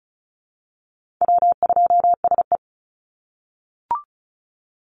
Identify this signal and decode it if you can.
Morse code, heard as a keyed tone, about 34 wpm, 700 Hz. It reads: W2SE
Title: NA SOTA Chaser Callsigns 35wpm